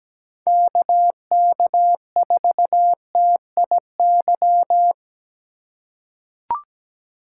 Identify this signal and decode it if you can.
Morse code, heard as a keyed tone, about 17 wpm, 700 Hz. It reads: KK4TIY
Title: USA Call Signs - 6 Characters - 2x Character Spacing - 17wpm